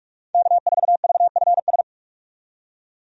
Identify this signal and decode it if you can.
Morse code, heard as a keyed tone, about 44 wpm, 700 Hz. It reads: K4VFS
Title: USA Call Signs - 5 Characters - 45wpm